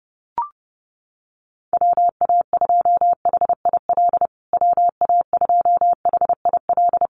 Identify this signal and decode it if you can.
Morse code, heard as a keyed tone, about 30 wpm, 700 Hz. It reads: WA2HIL WA2HIL
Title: USA Call Signs Repeated 3x 30wpm